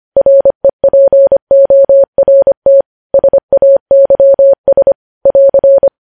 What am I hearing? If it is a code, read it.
Morse code, heard as a keyed tone, about 25 wpm, 550 Hz. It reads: REPORT SAYS <AR>
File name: podcast / News headlines-1x-25x25